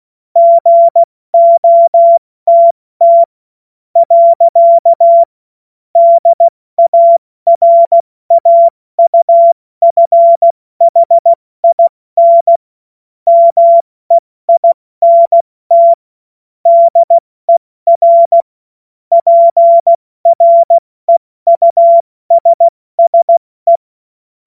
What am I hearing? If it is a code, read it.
Morse code, heard as a keyed tone, about 16 wpm, spaced as for 14 wpm, 680 Hz. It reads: GOTT. DARAUFHIN MEINT DER PREUSSE
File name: Witz-26-August-2025-70-F80